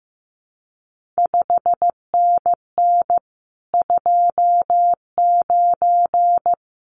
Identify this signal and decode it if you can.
Morse code, heard as a keyed tone, about 15 wpm, 700 Hz. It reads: 5NN 29
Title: CQ WW DX Contest Exchange Encore - 15wpm